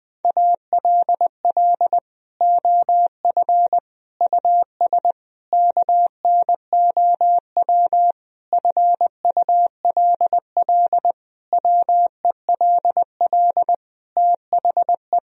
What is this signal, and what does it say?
Morse code, heard as a keyed tone, about 20 wpm, 700 Hz. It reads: ALL OF US KNOW FULL WELL THE
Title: Sentences from Top 1000 Words - Review Part 3 - 20wpm